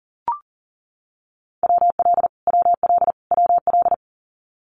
Morse code, heard as a keyed tone, about 40 wpm, 700 Hz. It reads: WL WL WL
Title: Abbreviations Repeated 3x - 40wpm